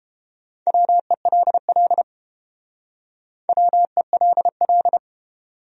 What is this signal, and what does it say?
Morse code, heard as a keyed tone, about 31 wpm, 700 Hz. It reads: WELL WELL